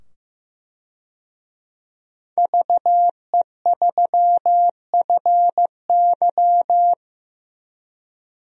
Morse code, heard as a keyed tone, about 15 wpm, 700 Hz. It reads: VE3FY
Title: DXCC-100 Call Signs - 5 Characters - 15wpm